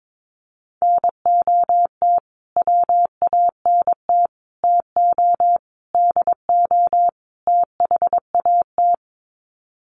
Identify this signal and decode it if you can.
Morse code, heard as a keyed tone, about 22 wpm, 700 Hz. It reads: NOT WANT TO DO THAT
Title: Sets of 5 Words Encore 22wpm